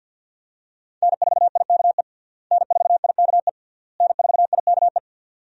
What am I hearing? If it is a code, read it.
Morse code, heard as a keyed tone, about 49 wpm, 700 Hz. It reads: N4IKE N4IKE N4IKE
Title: USA Call Signs - 5 Characters - Repeated 3x - 50wpm